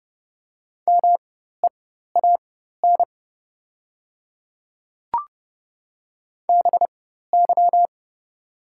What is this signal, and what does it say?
Morse code, heard as a keyed tone, about 30 wpm, spaced as for 14 wpm, 700 Hz. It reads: MEAN BY
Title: 4x Character Spacing - Top 200 Words - 30wpm